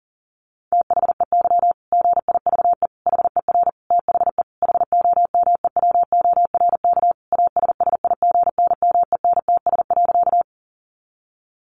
Morse code, heard as a keyed tone, about 37 wpm, 700 Hz. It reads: THEY GIVE HER THE HOMEWORK ASSIGNMENTS.
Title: Easy Phrases 40wpm